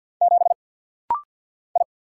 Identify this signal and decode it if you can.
Morse code, heard as a keyed tone, about 50 wpm, 700 Hz. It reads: 7 I
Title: Single Letter-Number - Rapid-Fire 50wpm